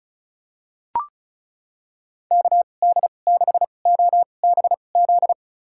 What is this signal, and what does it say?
Morse code, heard as a keyed tone, about 36 wpm, spaced as for 25 wpm, 700 Hz. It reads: KD6OBZ
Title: USA Call Signs - 6 Characters - 2x Character Spacing - 35wpm